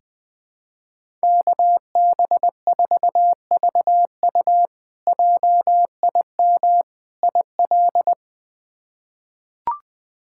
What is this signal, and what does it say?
Morse code, heard as a keyed tone, about 20 wpm, 700 Hz. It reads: KB4VU JIM IL
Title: K1USN SST QSO Practice - 20wpm